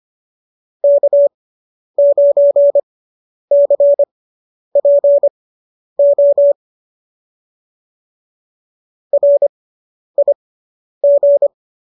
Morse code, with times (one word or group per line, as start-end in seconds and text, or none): 0.8-6.5 K9CPO
9.1-11.5 RIG